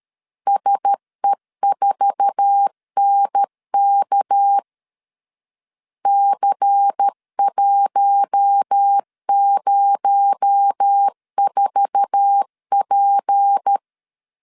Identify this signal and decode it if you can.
Morse code, heard as a keyed tone, about 13 wpm, 800 Hz. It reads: SE4NK C104P
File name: LEZIONI / 8 YJ45 / 8 YJ4 12wpm 2151